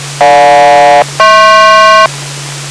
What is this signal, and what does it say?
Aircraft SELCAL, heard as two pairs of tones. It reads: GJ-HP